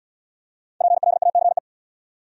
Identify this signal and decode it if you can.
Morse code, heard as a keyed tone, about 67 wpm, 700 Hz. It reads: PLACE